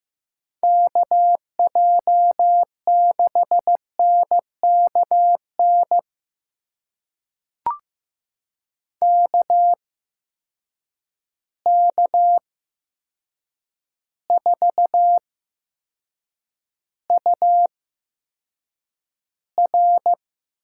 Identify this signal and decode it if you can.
Morse code, heard as a keyed tone, about 15 wpm, spaced as for 7 wpm, 700 Hz. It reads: KJ6NKN KK4UR
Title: USA Call Signs - 6 Characters - 8x Character Spacing - 15wpm